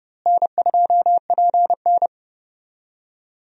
Morse code, heard as a keyed tone, about 30 wpm, 700 Hz. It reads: N2PN
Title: USA Call Signs - 4 Characters - Repeated 3x - 30wpm